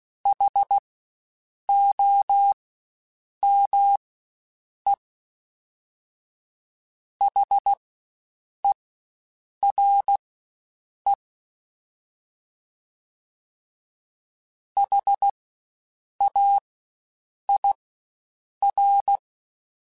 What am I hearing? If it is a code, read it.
Morse code, heard as a keyed tone, about 16 wpm, spaced as for 7 wpm, 800 Hz. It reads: HOME HERE HAIR
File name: Spaced Alphas / Spaced Alphas mp3s / Hs sp @15